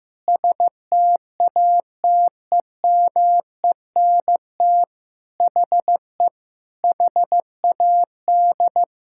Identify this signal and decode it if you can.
Morse code, heard as a keyed tone, about 15 wpm, 700 Hz. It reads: STATEMENT HE HAD
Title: Sentences from Top 1600 Words 15wpm